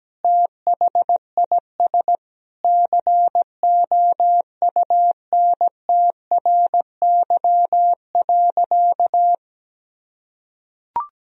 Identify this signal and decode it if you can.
Morse code, heard as a keyed tone, about 17 wpm, 700 Hz. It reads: THIS COUNTRY.
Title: Sentences from Top 1000 Words - Review Part 3 - 17wpm